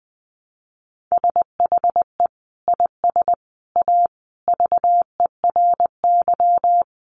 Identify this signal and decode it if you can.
Morse code, heard as a keyed tone, about 20 wpm, 700 Hz. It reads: SHE IS A VERY
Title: Sentences from Top 1100 Words 20wpm